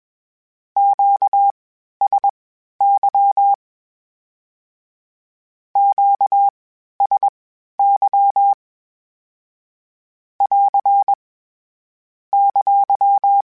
Morse code, heard as a keyed tone, about 21 wpm, spaced as for 12 wpm, 800 Hz. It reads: QSY QSY <AR> !
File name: (spaced) / Q-Calls Spaced-mp3s / Q Calls Twice sp @20